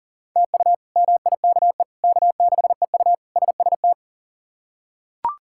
Morse code, heard as a keyed tone, about 37 wpm, 700 Hz. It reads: TU MIKE K6EU SST
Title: K1USN SST QSO Practice - 40wpm